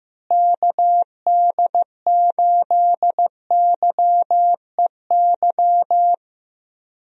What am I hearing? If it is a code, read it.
Morse code, heard as a keyed tone, about 15 wpm, 700 Hz. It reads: KD8YEY
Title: USA Call Signs - 6 Characters - Repeated 2x - 15wpm